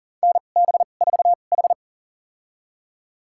Morse code, heard as a keyed tone, about 40 wpm, spaced as for 29 wpm, 700 Hz. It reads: NB4H
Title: USA Call Signs - 4 Characters - 2x Character Spacing - 40wpm